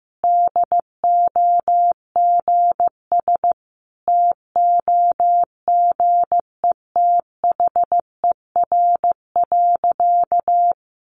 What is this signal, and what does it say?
Morse code, heard as a keyed tone, about 15 wpm, 700 Hz. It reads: DOGS TOGETHER.
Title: Easy Phrases 15wpm